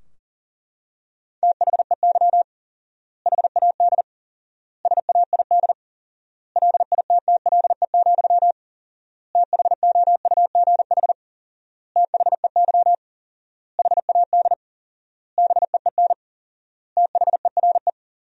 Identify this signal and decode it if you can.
Morse code, heard as a keyed tone, about 37 wpm, 700 Hz. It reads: THEY HAD SAID LITTLE, THOUGH THEY HAD BEEN THERE
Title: 4x Word-Spacing - Sentences from Top 400 Words - 40wpm